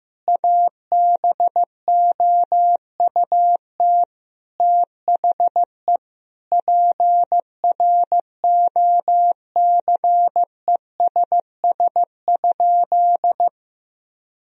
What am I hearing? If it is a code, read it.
Morse code, heard as a keyed tone, about 16 wpm, 700 Hz. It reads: ABOUT THE PROCESS?
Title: Sentences from Top 1000 Words - Review Part 3 - 15wpm